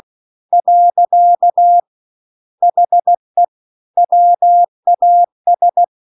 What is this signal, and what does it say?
Morse code, heard as a keyed tone, about 16 wpm, 700 Hz. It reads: . HE WAS
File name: CW-The-Big-Twoheaded-River-2-16WPM